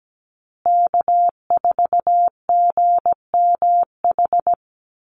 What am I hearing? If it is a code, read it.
Morse code, heard as a keyed tone, about 17 wpm, 700 Hz. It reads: K4GMH
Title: Call Signs Repeated 2x - 17wpm